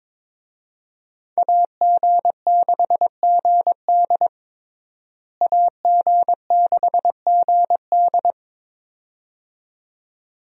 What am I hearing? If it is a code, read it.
Morse code, heard as a keyed tone, about 22 wpm, 700 Hz. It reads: AG6GD AG6GD
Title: USA Call Signs - 5 Characters - Repeated 2x - 22wpm